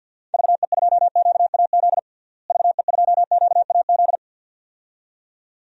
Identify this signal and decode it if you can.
Morse code, heard as a keyed tone, about 53 wpm, 700 Hz. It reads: VE2QAZ VE2QAZ
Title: DXCC-100 Call Signs - 6 Characters - Repeated 2x - 50wpm